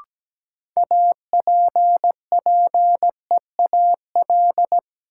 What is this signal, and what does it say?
Morse code, heard as a keyed tone, about 17 wpm, 700 Hz. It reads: APPEAL
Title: Sentences from Top 1300 Words - New Word Drill - 17wpm